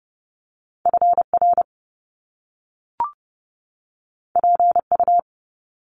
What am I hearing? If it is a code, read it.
Morse code, heard as a keyed tone, about 30 wpm, 700 Hz. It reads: FR PU
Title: Top Two Letter Combinations in Words 30wpm